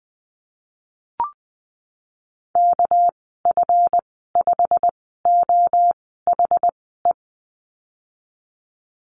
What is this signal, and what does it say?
Morse code, heard as a keyed tone, about 19 wpm, spaced as for 14 wpm, 700 Hz. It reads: KF5OHE
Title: 200 Lesson 32g - Letter k - Call Signs 20wpm